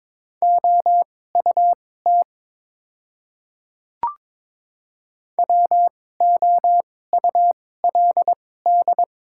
Morse code, heard as a keyed tone, about 22 wpm, spaced as for 16 wpm, 690 Hz. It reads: OUT WOULD
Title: 2x Character Spacing - Top 100 Words - 22wpm